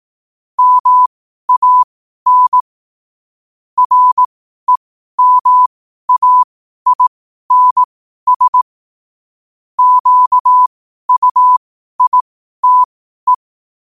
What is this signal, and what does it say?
Morse code, heard as a keyed tone, about 18 wpm, spaced as for 12 wpm, 1000 Hz. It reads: MAN REMAINS QUITE
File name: qotdfiles / 12wpm